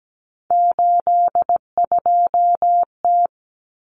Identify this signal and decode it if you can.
Morse code, heard as a keyed tone, about 17 wpm, 700 Hz. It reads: 82T
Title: Common Cut Numbers - 3-Digit 17wpm